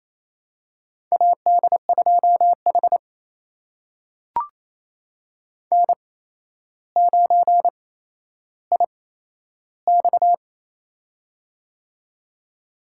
Morse code, heard as a keyed tone, about 27 wpm, spaced as for 12 wpm, 700 Hz. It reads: AD2H N9IX
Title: USA Call Signs - 4 Characters - 8x Character Spacing - 28wpm